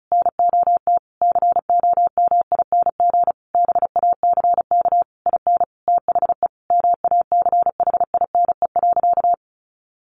Morse code, heard as a keyed tone, about 34 wpm, 700 Hz. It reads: NOT COMING BACK IN THE MACHINE.
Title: A Sound Of Thunder 35wpm